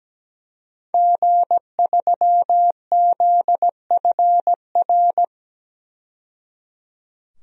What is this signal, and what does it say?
Morse code, heard as a keyed tone, about 17 wpm, 700 Hz. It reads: G3ZFR
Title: DXCC-100 Call Signs - 5 Characters - 17wpm